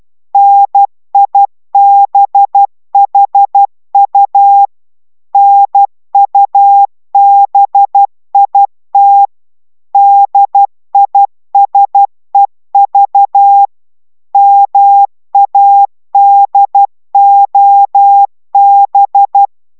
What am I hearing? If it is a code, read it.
Morse code, heard as a keyed tone, about 12 wpm, 800 Hz. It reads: NIBHU NUBIT DISEV MADOB